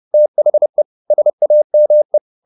Morse code, heard as a keyed tone, about 30 wpm, 600 Hz. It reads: THE SAME